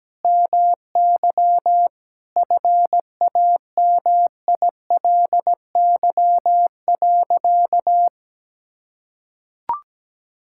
Morse code, heard as a keyed tone, about 17 wpm, 700 Hz. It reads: MY FAMILY.
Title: Sentences from Top 1600 Words 17wpm